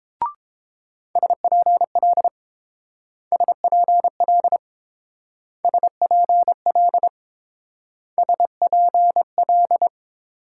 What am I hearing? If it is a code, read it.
Morse code, heard as a keyed tone, about 27 wpm, 700 Hz. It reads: SPL SPL SPL SPL
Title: Speed-Racing Multi-letter Phonemes - 22wpm